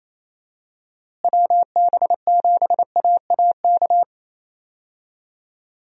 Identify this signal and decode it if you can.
Morse code, heard as a keyed tone, about 28 wpm, 700 Hz. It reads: WB7AAK
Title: USA Call Signs - 6 Characters - Repeated 2x - 28wpm